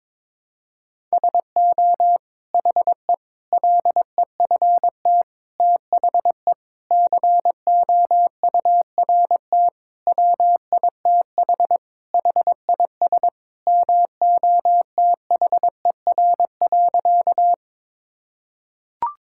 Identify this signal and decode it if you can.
Morse code, heard as a keyed tone, about 22 wpm, 700 Hz. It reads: SO HE LEFT THE COURT WITH HIS MOTHER.
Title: Sentences from Top 1600 Words 22wpm